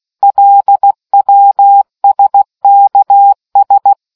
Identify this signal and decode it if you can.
Morse code, heard as a keyed tone, about 16 wpm, 800 Hz. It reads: LWSKS